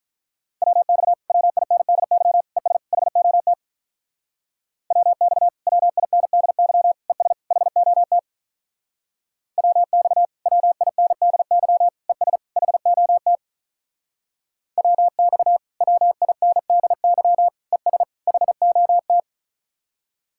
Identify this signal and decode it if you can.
Morse code, heard as a keyed tone, about 46 wpm, 700 Hz. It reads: WX WINDY ES HOT WX WINDY ES HOT WX WINDY ES HOT WX WINDY ES HOT